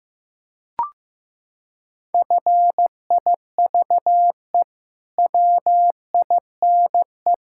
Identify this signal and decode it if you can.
Morse code, heard as a keyed tone, about 15 wpm, 700 Hz. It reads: FIVE WINE
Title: Pangram Challenge v2 - 15wpm